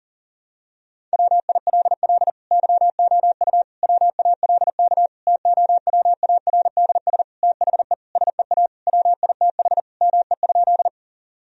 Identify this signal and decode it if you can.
Morse code, heard as a keyed tone, about 40 wpm, 700 Hz. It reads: WILL YOU WALK TOWARDS THE SEA WITH ME?
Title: Sentences from Top 1000 Words - Review Part 2 - 40wpm